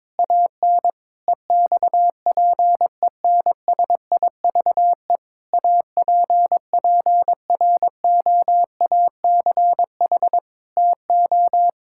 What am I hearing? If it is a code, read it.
Morse code, heard as a keyed tone, about 22 wpm, 700 Hz. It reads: AN EXPENSIVE APPROACH TO
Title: Sentences from Top 1000 Words - Review Part 6 - 22wpm